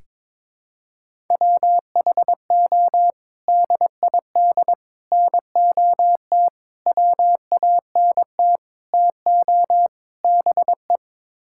Morse code, heard as a keyed tone, about 22 wpm, 700 Hz. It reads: WHO DID NOT WANT TO BE
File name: practice / 22wpm / Sets of 6 Words - 22wpm